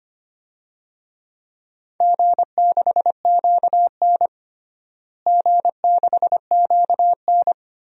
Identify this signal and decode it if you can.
Morse code, heard as a keyed tone, about 25 wpm, 700 Hz. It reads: G6QN G6QN
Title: DXCC-100 Call Signs - 4 Characters - Repeated 3x - 25wpm